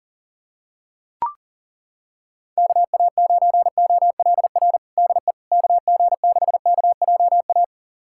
Morse code, heard as a keyed tone, about 40 wpm, 700 Hz. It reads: KA9OLR DE KG6KJA
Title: USA Callsign DE Callsign - 6 Characters - 40wpm